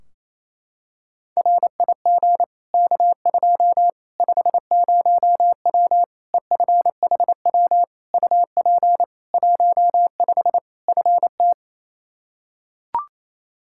Morse code, heard as a keyed tone, about 28 wpm, 700 Hz. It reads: RIG K2 50W EFHW UP 15 FT
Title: Practice QSO - Exchange 2 - Info - 28wpm